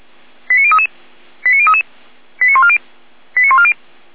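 Five-tone selective call (ZVEI), heard as quick runs of five tones, tones of about 70 ms. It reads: 89033 89033 89140 89140
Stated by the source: ZVEI five-tone selective call received on SDR.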